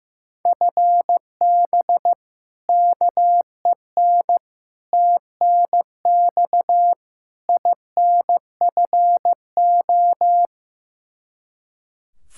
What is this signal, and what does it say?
Morse code, heard as a keyed tone, about 15 wpm, 700 Hz. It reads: FB KEN TNX INFO
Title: Practice QSO - Exchange 3 - More Info - 15wpm